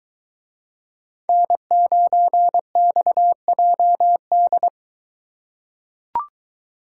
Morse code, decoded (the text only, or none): N9XJD